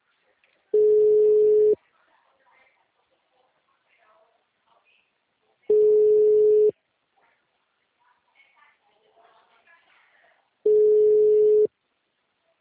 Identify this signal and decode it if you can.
Ringback tone (Europe, 425 Hz), with speech.
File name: out- -2011- - - .41860